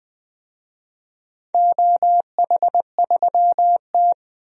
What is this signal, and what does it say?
Morse code, heard as a keyed tone, about 21 wpm, 700 Hz. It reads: OH3T
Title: DXCC-100 Call Signs - 4 Characters - 20wpm